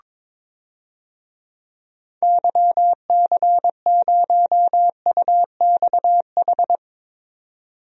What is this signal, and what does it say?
Morse code, heard as a keyed tone, about 22 wpm, 700 Hz. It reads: YC0UXH
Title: DXCC-100 Call Signs - 6 Characters - 22wpm